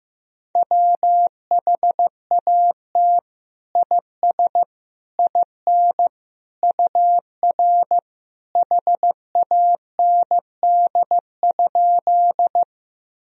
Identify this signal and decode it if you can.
Morse code, heard as a keyed tone, about 15 wpm, 700 Hz. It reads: WHAT IS IN UR HAND?
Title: Sentences from Top 200 Words and Abbreviations 15wpm